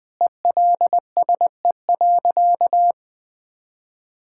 Morse code, heard as a keyed tone, about 20 wpm, 700 Hz. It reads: ELSE.
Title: Sentences with 5-Word Phrases - Spoken Beginning - 20wpm